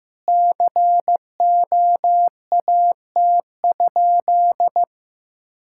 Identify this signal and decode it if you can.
Morse code, heard as a keyed tone, about 15 wpm, 700 Hz. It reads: COAT?
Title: Sentences from Top 1600 Words 15wpm